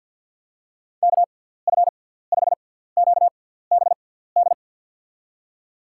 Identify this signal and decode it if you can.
Morse code, heard as a keyed tone, about 52 wpm, spaced as for 17 wpm, 700 Hz. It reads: KF5QBD
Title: USA Call Signs - 6x Character Spacing - 50wpm